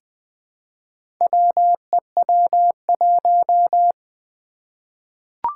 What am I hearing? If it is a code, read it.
Morse code, heard as a keyed tone, about 20 wpm, 700 Hz. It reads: WEW1
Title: Four Challenging Letter-Number Combinations 20wpm